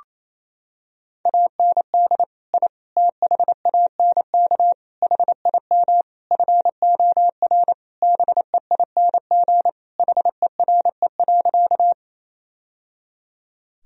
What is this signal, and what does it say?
Morse code, heard as a keyed tone, about 28 wpm, 700 Hz. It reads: AND I THANK HIM FOR BEING HERE.
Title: Sentences from Top 1000 Words - Review Part 3 - 28wpm